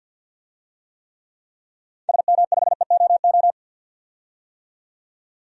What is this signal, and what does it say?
Morse code, heard as a keyed tone, about 46 wpm, 700 Hz. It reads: SM5EOO
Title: DXCC-100 Call Signs - 6 Characters - 50wpm